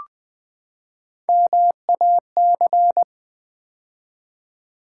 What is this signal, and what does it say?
Morse code, heard as a keyed tone, about 20 wpm, 700 Hz. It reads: MAC
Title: Build A Word - 3 Letters At A Time - 9-Letter Words - 20wpm